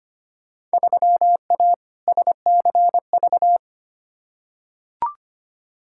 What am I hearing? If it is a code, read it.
Morse code, heard as a keyed tone, about 25 wpm, 700 Hz. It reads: 3A SCV
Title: Field Day Exchanges - 25wpm